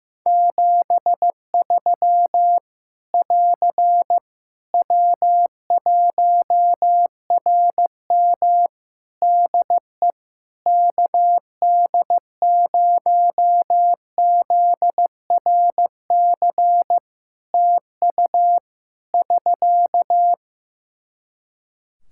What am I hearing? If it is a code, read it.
Morse code, heard as a keyed tone, about 15 wpm, 700 Hz. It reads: 73 <AR> W1RM DE KD0ZRC TU <SK>
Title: Practice QSO - Exchange 4 - Ending - 15wpm